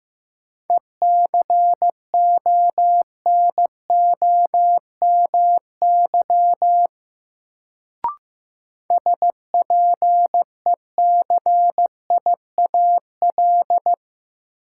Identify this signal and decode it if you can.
Morse code, heard as a keyed tone, about 15 wpm, 700 Hz. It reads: ECONOMY SPECIAL
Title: Sentences from Top 1200 Words - New Word Drill Alt - 15wpm